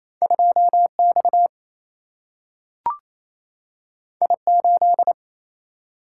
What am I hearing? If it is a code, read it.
Morse code, heard as a keyed tone, about 28 wpm, 700 Hz. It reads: 2 X I 8
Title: Single-Letter Single-Number 28wpm